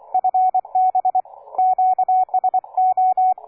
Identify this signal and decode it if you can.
Morse code, heard as a keyed tone, about 27 wpm, 740 Hz. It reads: FB QSO
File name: cw forum / K9DX 1818 le 17-12-2008 05h47